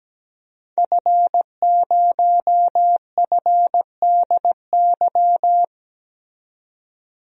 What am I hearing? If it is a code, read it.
Morse code, heard as a keyed tone, about 17 wpm, 700 Hz. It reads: F0FDY
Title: DXCC-100 Call Signs - 5 Characters - 17wpm